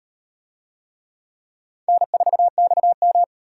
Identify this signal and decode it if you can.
Morse code, heard as a keyed tone, about 36 wpm, 700 Hz. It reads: N4XM